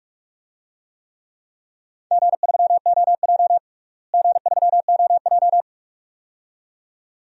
Morse code, heard as a keyed tone, about 42 wpm, 700 Hz. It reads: G3OJ G3OJ
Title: DXCC-100 Call Signs - 4 Characters - Repeated 2x - 45wpm